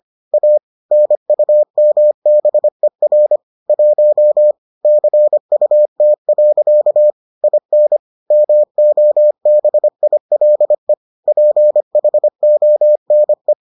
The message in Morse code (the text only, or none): A NUMBER 1 CUT. IN MOBILE PHONE